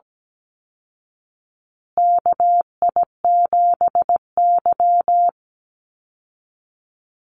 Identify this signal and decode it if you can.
Morse code, heard as a keyed tone, about 17 wpm, 700 Hz. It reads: KI7Y
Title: Call Signs Repeated 2x - 17wpm